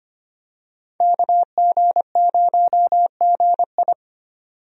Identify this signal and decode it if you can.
Morse code, heard as a keyed tone, about 25 wpm, 700 Hz. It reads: KG0GI
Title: USA Call Signs - 5 Characters - 25wpm